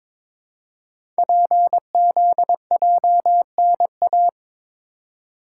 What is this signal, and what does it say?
Morse code, heard as a keyed tone, about 23 wpm, 700 Hz. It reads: PZJNA
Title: Code Groups - 5 Letters - 22wpm